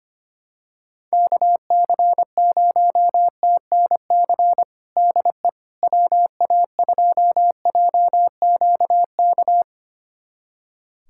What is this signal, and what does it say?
Morse code, heard as a keyed tone, about 25 wpm, 700 Hz. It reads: KC0TNC DE WA2JQK
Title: USA Callsign DE Callsign - 6 Characters - 25wpm